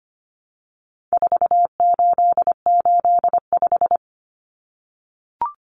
Morse code, heard as a keyed tone, about 25 wpm, 700 Hz. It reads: 4885
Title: Common Cut Numbers - 4-Digit 25wpm